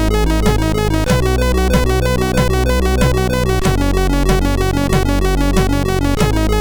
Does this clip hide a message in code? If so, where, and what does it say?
No.